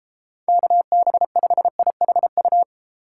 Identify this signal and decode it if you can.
Morse code, heard as a keyed tone, about 31 wpm, 700 Hz. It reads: KB5IHU